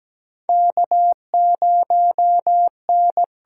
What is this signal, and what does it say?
Morse code, heard as a keyed tone, about 17 wpm, 700 Hz. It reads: K0N
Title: USA Call Signs - 4 Characters - 8x Character Spacing - 17wpm